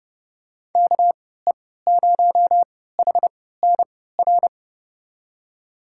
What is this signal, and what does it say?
Morse code, heard as a keyed tone, about 30 wpm, spaced as for 17 wpm, 700 Hz. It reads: KE0HNR
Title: USA Call Signs - 6 Characters - 3x Character Spacing - 30wpm